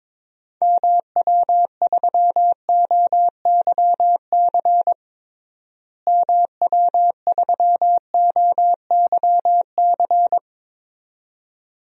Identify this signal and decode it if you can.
Morse code, heard as a keyed tone, about 22 wpm, 700 Hz. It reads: MW3OYC MW3OYC
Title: DXCC-100 Call Signs - 6 Characters - Repeated 2x - 22wpm